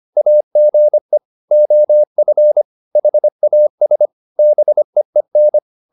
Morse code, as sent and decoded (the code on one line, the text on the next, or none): .- --. / --- ..-. / .... .- ... / -... . -.
AGE OF HAS BEEN